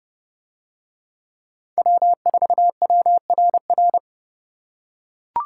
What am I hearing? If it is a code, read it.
Morse code, heard as a keyed tone, about 29 wpm, 700 Hz. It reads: W4WRR